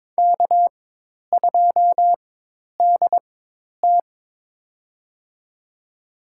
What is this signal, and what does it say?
Morse code, heard as a keyed tone, about 22 wpm, spaced as for 10 wpm, 700 Hz. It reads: K2DT
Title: USA Call Signs - 4 Characters - 4x Character Spacing - 22wpm